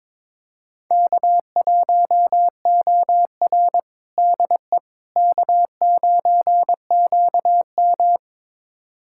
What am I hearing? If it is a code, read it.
Morse code, heard as a keyed tone, about 22 wpm, 700 Hz. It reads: K1OR DE K9QM
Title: USA Callsign DE Callsign - 4 Characters - 22wpm